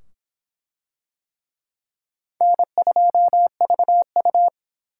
Morse code, heard as a keyed tone, about 26 wpm, 700 Hz. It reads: N2VU